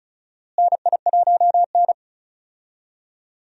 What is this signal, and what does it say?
Morse code, heard as a keyed tone, about 34 wpm, 700 Hz. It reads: NI1N
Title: USA Call Signs - 4 Characters - Repeated 3x - 35wpm